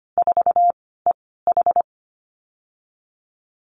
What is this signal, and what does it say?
Morse code, heard as a keyed tone, about 25 wpm, 700 Hz. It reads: 4 E H
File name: Lesson 15g - Number 5 - Call Signs 25wpm